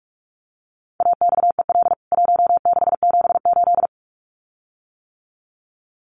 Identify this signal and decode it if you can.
Morse code, heard as a keyed tone, about 44 wpm, 700 Hz. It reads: AXEL 1678
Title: CWT Exchange 45wpm